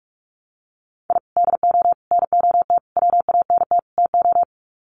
Morse code, heard as a keyed tone, about 44 wpm, 700 Hz. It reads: I DO NOT WANT TO
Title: Sets of 5 Words Encore2 45wpm